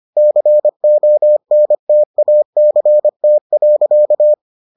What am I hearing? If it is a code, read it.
Morse code, heard as a keyed tone, about 25 wpm, 600 Hz. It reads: CONTACT.